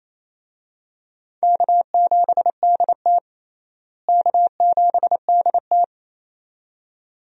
Morse code, heard as a keyed tone, about 28 wpm, 700 Hz. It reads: K7DT K7DT
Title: USA Call Signs - 4 Characters - Repeated 2x - 28wpm